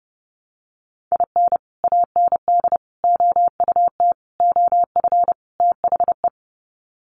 Morse code, heard as a keyed tone, about 30 wpm, 700 Hz. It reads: IN AND OUT OF THE
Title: Sets of 5 Words 30wpm